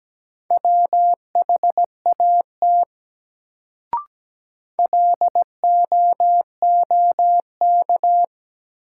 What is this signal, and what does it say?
Morse code, heard as a keyed tone, about 17 wpm, 700 Hz. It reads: WHAT LOOK
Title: Sentences from Top 100 Words - New Word Drill - 17wpm